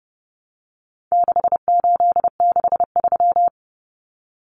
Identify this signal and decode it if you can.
Morse code, heard as a keyed tone, about 30 wpm, 700 Hz. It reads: 6863
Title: Common Cut Numbers - 4-Digit 30wpm